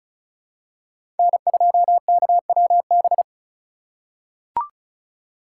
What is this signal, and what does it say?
Morse code, heard as a keyed tone, about 36 wpm, 700 Hz. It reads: N2KWB
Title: USA Call Signs - 5 Characters - 4x Character Spacing - 35wpm